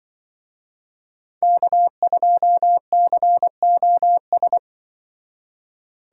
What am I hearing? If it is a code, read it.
Morse code, heard as a keyed tone, about 24 wpm, 700 Hz. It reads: K2COS
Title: Speed Racing - USA 1x3 Call Signs 20wpm